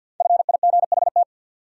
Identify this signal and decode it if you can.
Morse code, heard as a keyed tone, about 52 wpm, 700 Hz. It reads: FIGHT
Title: Five Letter Words - 50wpm